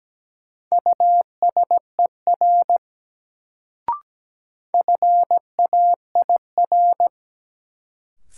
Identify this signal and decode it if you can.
Morse code, heard as a keyed tone, about 17 wpm, 700 Hz. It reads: USER FAIR
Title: Sentences from Top 1000 Words - Review Part 5 - New Word Drill - 17wpm